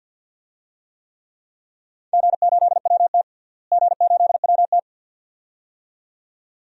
Morse code, heard as a keyed tone, about 50 wpm, 700 Hz. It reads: G8WT G8WT
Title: DXCC-100 Call Signs - 4 Characters - Repeated 2x - 50wpm